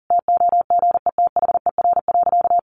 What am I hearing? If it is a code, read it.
Morse code, heard as a keyed tone, about 40 wpm, 700 Hz. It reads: TOGETHER.
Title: Sentences from Top 1300 Words 40wpm